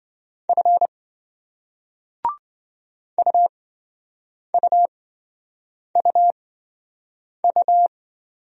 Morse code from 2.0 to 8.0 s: UUUU

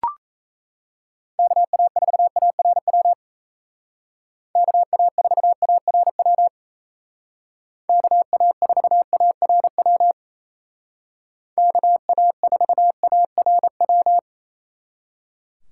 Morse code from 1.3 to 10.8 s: KA4ARW KA4ARW KA4ARW

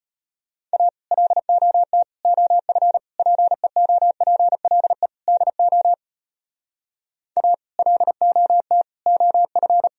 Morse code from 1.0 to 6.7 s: LOT OF PEOPLE DO

